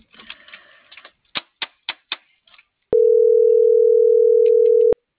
Ringback tone (North America), 440 + 480 Hz.